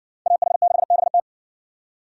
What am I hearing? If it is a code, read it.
Morse code, heard as a keyed tone, about 57 wpm, 700 Hz. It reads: AH6BT